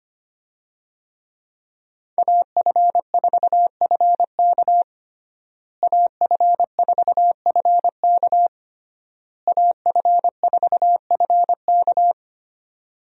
Morse code, heard as a keyed tone, about 25 wpm, 700 Hz. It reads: AF4FK AF4FK AF4FK
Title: USA Call Signs - 5 Characters - Repeated 3x - 25wpm